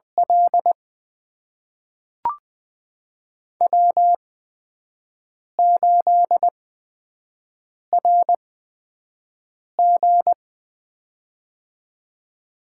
Morse code, heard as a keyed tone, about 20 wpm, spaced as for 5 wpm, 700 Hz. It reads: L W8RG
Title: USA Call Signs - 4 Characters - 8x Character Spacing - 20wpm